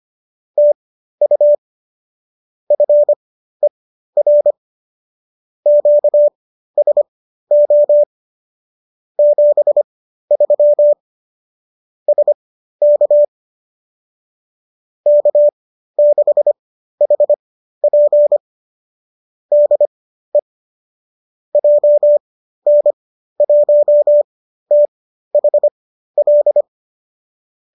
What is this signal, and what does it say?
Morse code, heard as a keyed tone, about 25 wpm, spaced as for 13 wpm, 590 Hz. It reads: TU FER QSO 73 SK K6HP DE JN1THL